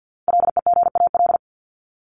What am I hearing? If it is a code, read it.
Morse code, heard as a keyed tone, about 50 wpm, 700 Hz. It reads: LEGAL
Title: Top 1001-1250 Words 50wpm